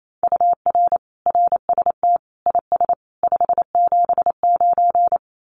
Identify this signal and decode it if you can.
Morse code, heard as a keyed tone, about 28 wpm, 700 Hz. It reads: UR RST IS 579
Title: Common QSO Phrases 28wpm